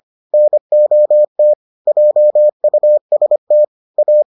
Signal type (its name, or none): Morse code